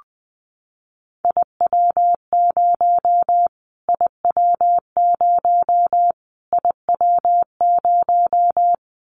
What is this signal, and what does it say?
Morse code, heard as a keyed tone, about 20 wpm, 700 Hz. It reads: IW0 IW0 IW0
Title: Callsign Prefixes for Top 25 DXCC - Repeated 3x - 20wpm